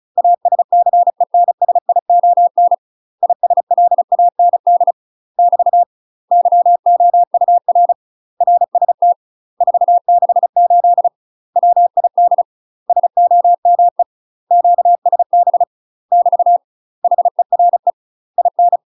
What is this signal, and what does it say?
Morse code, heard as a keyed tone, about 34 wpm, 700 Hz. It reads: ASCENSION ISLAND <BT> YOUR RST 468 WID SOME QSB <BT> HERE IN
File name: UK MASTER QSOs TRAINING/35WPM